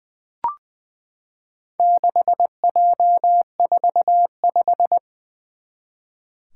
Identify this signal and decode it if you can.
Morse code, heard as a keyed tone, about 20 wpm, 700 Hz. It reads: 6J45
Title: Four Challenging Letter-Number Combinations 20wpm